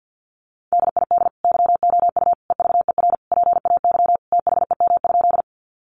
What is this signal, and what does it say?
Morse code, heard as a keyed tone, about 50 wpm, 700 Hz. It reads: DID YOU EVER PAY THEM?
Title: Sentences from Top 700 Words 50wpm